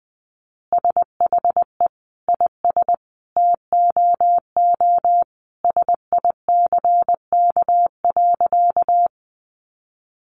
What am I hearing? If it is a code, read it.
Morse code, heard as a keyed tone, about 20 wpm, 700 Hz. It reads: SHE IS TOO SICK.
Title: Easy Phrases 20wpm